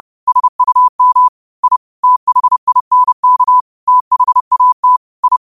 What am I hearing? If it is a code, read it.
Morse code, heard as a keyed tone, about 30 wpm, 1000 Hz. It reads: SUM I THINK THAT I
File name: qotdfiles / 30wpm